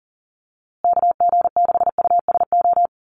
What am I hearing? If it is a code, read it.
Morse code, heard as a keyed tone, about 40 wpm, 700 Hz. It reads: KG6USO
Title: USA Call Signs Repeated 3x 40wpm